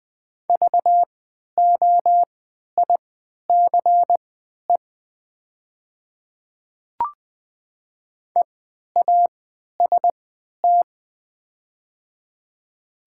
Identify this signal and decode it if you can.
Morse code, heard as a keyed tone, about 20 wpm, spaced as for 11 wpm, 700 Hz. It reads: VOICE EAST
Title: 3x Character Spacing - Top 500 Words - 20wpm